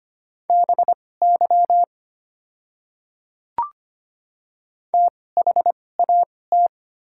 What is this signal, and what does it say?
Morse code, heard as a keyed tone, about 25 wpm, spaced as for 18 wpm, 700 Hz. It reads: BY THAT
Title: 2x Character Spacing - Top 100 Words - 25wpm